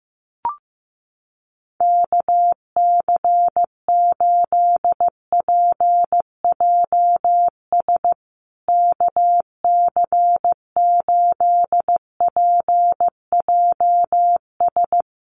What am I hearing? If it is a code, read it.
Morse code, heard as a keyed tone, about 14 wpm, 700 Hz. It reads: KC8PJS KC8PJS
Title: USA Call Signs Repeated 3x 15wpm